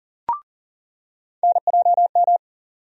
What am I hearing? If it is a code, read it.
Morse code, heard as a keyed tone, about 40 wpm, 700 Hz. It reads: NJM
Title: USA Call Sign Postfixes - 3 Letters 40wpm